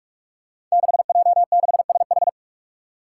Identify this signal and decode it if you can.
Morse code, heard as a keyed tone, about 41 wpm, 700 Hz. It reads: 6J6SH
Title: Five Challenging Letter-Number Combinations 45wpm